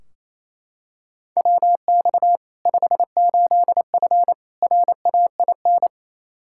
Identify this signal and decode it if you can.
Morse code, heard as a keyed tone, about 28 wpm, 700 Hz. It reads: WX 58F RAIN
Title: Practice QSO - Exchange 2 - Info - 28wpm